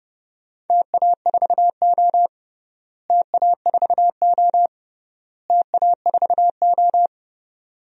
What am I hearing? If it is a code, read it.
Morse code, heard as a keyed tone, about 30 wpm, 700 Hz. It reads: TA4O TA4O TA4O
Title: DXCC-100 Call Signs - 4 Characters - Repeated 3x - 30wpm